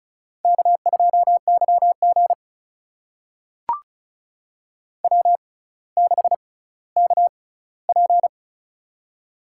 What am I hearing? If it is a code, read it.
Morse code, heard as a keyed tone, about 34 wpm, spaced as for 18 wpm, 700 Hz. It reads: K2YG W6KP
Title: USA Call Signs - 4 Characters - 6x Character Spacing - 35wpm